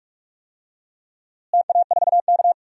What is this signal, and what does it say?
Morse code, heard as a keyed tone, about 46 wpm, 700 Hz. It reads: TA4K
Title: DXCC-100 Call Signs - 4 Characters - Repeated 2x - 45wpm